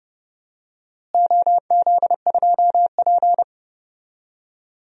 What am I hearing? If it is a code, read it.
Morse code, heard as a keyed tone, about 30 wpm, 700 Hz. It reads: OZ2P
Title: DXCC-100 Call Signs - 4 Characters - Repeated 3x - 30wpm